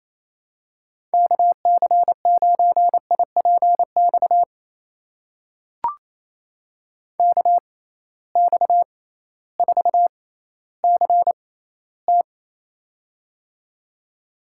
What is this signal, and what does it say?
Morse code, heard as a keyed tone, about 28 wpm, spaced as for 15 wpm, 700 Hz. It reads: KC9IPX KX4CT
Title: USA Call Signs - 6x Character Spacing - 28wpm